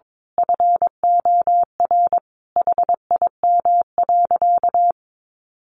Morse code, heard as a keyed tone, about 22 wpm, 700 Hz. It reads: FOR HIM.
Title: Sentences from Top 500 Words 22wpm